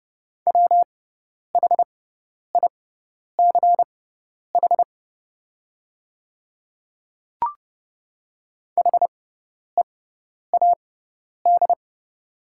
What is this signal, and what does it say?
Morse code, heard as a keyed tone, about 30 wpm, spaced as for 10 wpm, 700 Hz. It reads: WHICH HEAD
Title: 6x Character Spacing - Top 500 Words - 30wpm